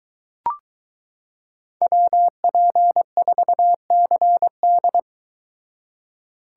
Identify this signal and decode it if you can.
Morse code, heard as a keyed tone, about 22 wpm, 700 Hz. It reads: WP4CD